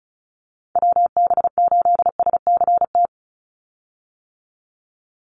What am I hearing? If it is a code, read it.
Morse code, heard as a keyed tone, about 36 wpm, 700 Hz. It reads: WB8SCT
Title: Call Signs - 35wpm